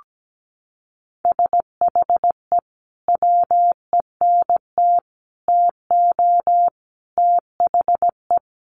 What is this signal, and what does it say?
Morse code, heard as a keyed tone, about 17 wpm, 700 Hz. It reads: SHE WENT TO THE
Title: Sentences from Top 900 Words 17wpm